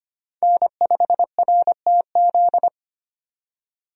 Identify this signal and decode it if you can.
Morse code, heard as a keyed tone, about 25 wpm, 700 Hz. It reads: N5RTZ
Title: USA Call Signs - 5 Characters - 25wpm